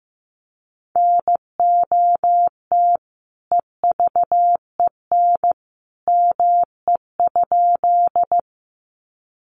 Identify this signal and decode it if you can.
Morse code, heard as a keyed tone, about 15 wpm, 700 Hz. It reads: NOT EVEN ME?
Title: Sentences from Top 600 Words 15wpm